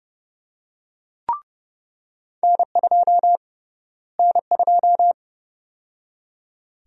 Morse code, heard as a keyed tone, about 30 wpm, 700 Hz. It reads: N2 N2
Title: USA Call Sign Prefixes - Repeated 2x - 30wpm